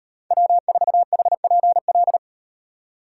Morse code, heard as a keyed tone, about 35 wpm, 700 Hz. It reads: W4HPL